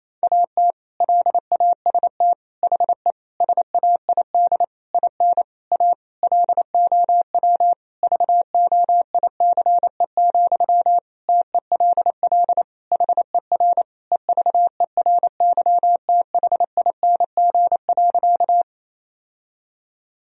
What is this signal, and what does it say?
Morse code, heard as a keyed tone, about 28 wpm, 700 Hz. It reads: AT LAST HE SAID IN A LOW VOICE, TELL HER EVERYTHING.
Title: Sentences from Top 1000 Words - Review Part 1 - 28wpm